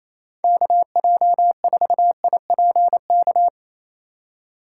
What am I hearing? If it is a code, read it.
Morse code, heard as a keyed tone, about 28 wpm, 700 Hz. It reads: KJ4IPK
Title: USA Call Signs - 4x Character Spacing - 28wpm